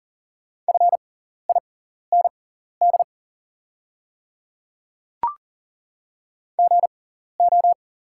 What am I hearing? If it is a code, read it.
Morse code, heard as a keyed tone, about 40 wpm, spaced as for 14 wpm, 700 Hz. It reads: FIND GO
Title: 6x Character Spacing - Top 100 Words - 40wpm